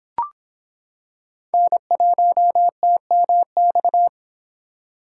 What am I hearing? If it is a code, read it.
Morse code, heard as a keyed tone, about 25 wpm, 700 Hz. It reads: N1TMX